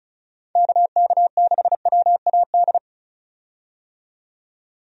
Morse code, heard as a keyed tone, about 36 wpm, 700 Hz. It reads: KK6WAD
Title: USA Call Signs - 6 Characters - 35wpm